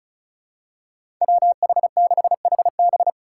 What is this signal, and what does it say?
Morse code, heard as a keyed tone, about 34 wpm, 700 Hz. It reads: WH6HB
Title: USA Call Signs - 2x Character Spacing - 35wpm